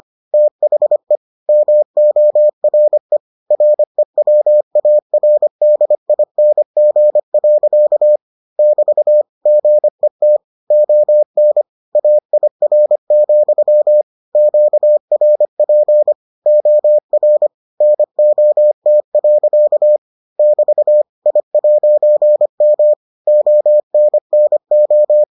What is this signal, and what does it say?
Morse code, heard as a keyed tone, about 25 wpm, 600 Hz. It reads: THE MORE REWARDING. <BT> GET ON AIR, QRP OR NOT. <BT> I'M ONNO